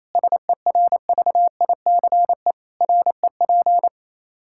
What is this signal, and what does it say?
Morse code, heard as a keyed tone, about 28 wpm, 700 Hz. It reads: SERVICE REP
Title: Practice QSO - Exchange 3 - More Info - 28wpm